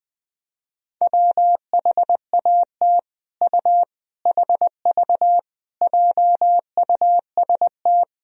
Morse code, heard as a keyed tone, about 20 wpm, 700 Hz. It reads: WHAT U HV JUST
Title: Sentences from Top 200 Words and Abbreviations 20wpm